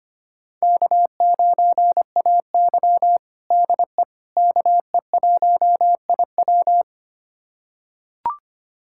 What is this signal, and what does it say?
Morse code, heard as a keyed tone, about 25 wpm, 700 Hz. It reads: K9AY DE KE1IW
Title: USA Callsign DE Callsign 25wpm